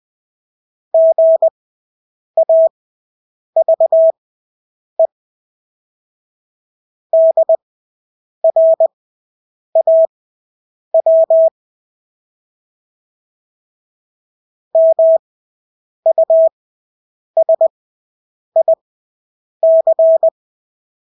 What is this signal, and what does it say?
Morse code, heard as a keyed tone, about 20 wpm, spaced as for 8 wpm, 650 Hz. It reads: GAVE DRAW MUSIC
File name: Engelske ord 20/8 wpm